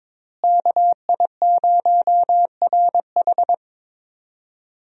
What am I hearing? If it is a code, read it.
Morse code, heard as a keyed tone, about 22 wpm, 700 Hz. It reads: KI0RH
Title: USA Call Signs - 5 Characters - Repeated 2x - 22wpm